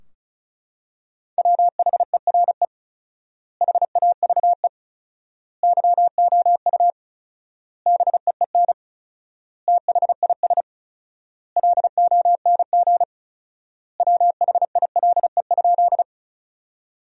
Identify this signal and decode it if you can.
Morse code, heard as a keyed tone, about 36 wpm, 700 Hz. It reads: WHERE HAVE YOU BEEN THIS LONG WHILE?
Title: 4x Word-Spacing - Sentences from Top 300 Words - 35wpm